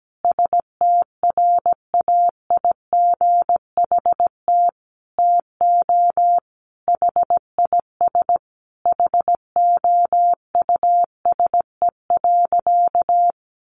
Morse code, heard as a keyed tone, about 17 wpm, 700 Hz. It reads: STRAIGHT TO HIS HOUSE.
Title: Sentences from Top 1000 Words 17wpm